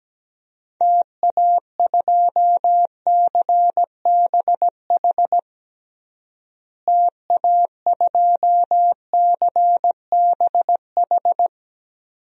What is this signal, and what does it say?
Morse code, heard as a keyed tone, about 17 wpm, 700 Hz. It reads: TA2CBH TA2CBH
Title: DXCC-100 Call Signs - 6 Characters - Repeated 2x - 17wpm